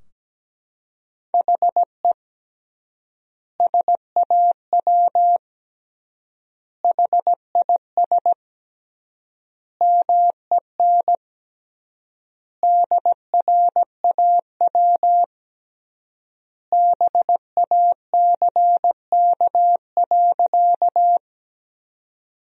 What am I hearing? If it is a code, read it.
Morse code, heard as a keyed tone, about 17 wpm, 700 Hz. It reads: HE SAW HIS MEN DRAW BACK.
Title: 3x Word-Spacing - Sentences from Top 300 Words - 17wpm